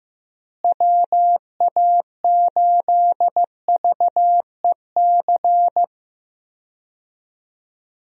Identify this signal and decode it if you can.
Morse code, heard as a keyed tone, about 15 wpm, 700 Hz. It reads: WA8VEC
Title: USA Call Signs - 6 Characters - Repeated 2x - 15wpm